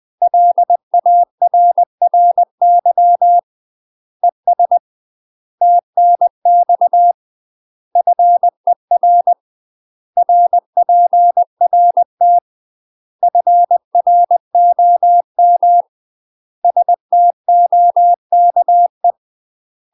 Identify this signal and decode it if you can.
Morse code, heard as a keyed tone, about 20 wpm, 700 Hz. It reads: LARRY ES TNX FER RPRT FROM STOKE
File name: UK MASTER QSOs TRAINING / 20WPM / UK MASTER QSOs 0011